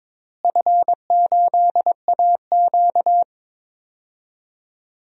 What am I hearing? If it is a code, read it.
Morse code, heard as a keyed tone, about 22 wpm, 700 Hz. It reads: F8AQ
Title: DXCC-100 Call Signs - 4 Characters - Repeated 2x - 22wpm